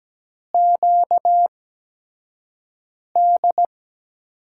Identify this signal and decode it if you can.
Morse code, heard as a keyed tone, about 17 wpm, spaced as for 5 wpm, 700 Hz. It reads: QD
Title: USA Call Signs - 5 Characters - 8x Character Spacing - 17wpm